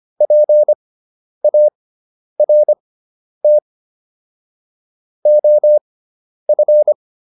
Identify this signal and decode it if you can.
Morse code, heard as a keyed tone, about 25 wpm, spaced as for 10 wpm, 600 Hz. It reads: PART OF